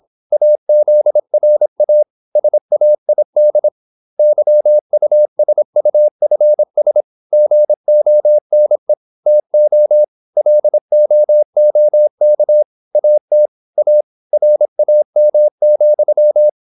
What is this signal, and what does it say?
Morse code, heard as a keyed tone, about 26 wpm, 600 Hz. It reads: AZRA SAID YUSUFS GONE TO LOOK AT A RAM,
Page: Never 26wpm | CW med Gnister